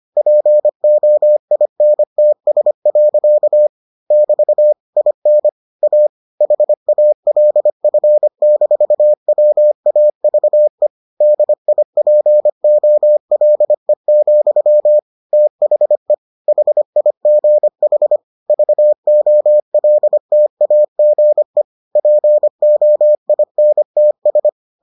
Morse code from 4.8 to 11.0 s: IN A HALF-WAVE

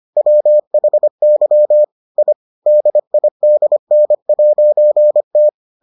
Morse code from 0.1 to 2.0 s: WHY

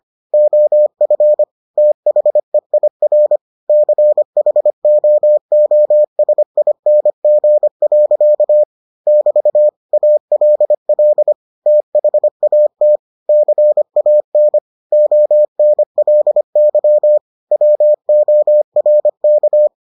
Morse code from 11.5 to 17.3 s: THAT CAN ONLY

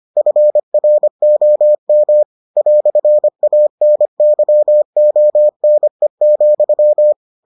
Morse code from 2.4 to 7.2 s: "ANYONE,

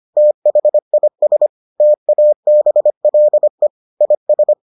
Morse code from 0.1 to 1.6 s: THIS

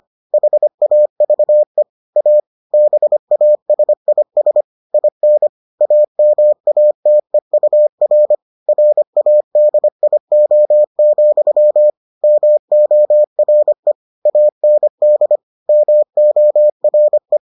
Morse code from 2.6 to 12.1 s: BASIS IN AMATEUR RADIO,

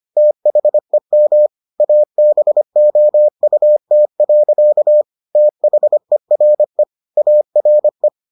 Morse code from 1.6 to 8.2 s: ABOUT. THERE ARE

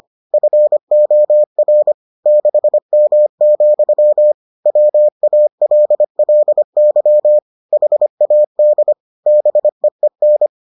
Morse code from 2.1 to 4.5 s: 6M,

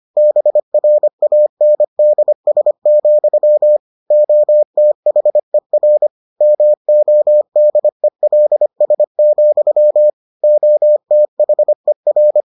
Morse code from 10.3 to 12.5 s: OTHER

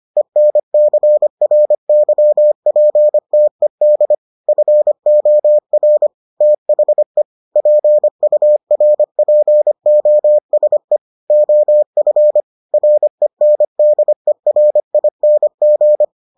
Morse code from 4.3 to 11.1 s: FOR THE PURPOSE

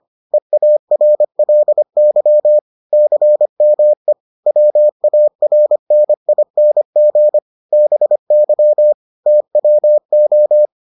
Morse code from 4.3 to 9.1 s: WARNING BY

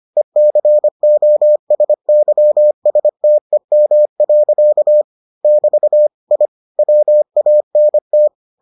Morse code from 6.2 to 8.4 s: I WANT

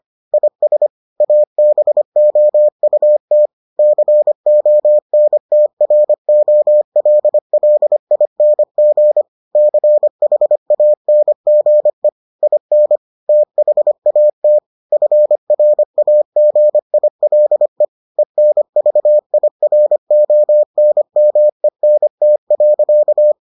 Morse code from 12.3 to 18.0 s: IN THAT FRAGILE

